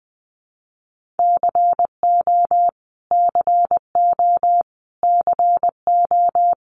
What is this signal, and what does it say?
Morse code, heard as a keyed tone, about 20 wpm, 700 Hz. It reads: CO CO CO
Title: US States Repeated 3x 20wpm